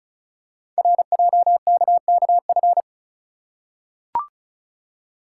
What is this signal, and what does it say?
Morse code, heard as a keyed tone, about 36 wpm, 690 Hz. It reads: RJKKF
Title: Code Groups - 5 Letters - 35wpm